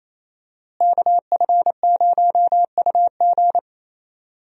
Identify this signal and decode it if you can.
Morse code, heard as a keyed tone, about 28 wpm, 700 Hz. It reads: KF0UG
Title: USA Call Signs - 5 Characters - Repeated 3x - 28wpm